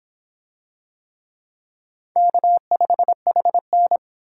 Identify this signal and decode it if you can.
Morse code, heard as a keyed tone, about 26 wpm, 700 Hz. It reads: K5HN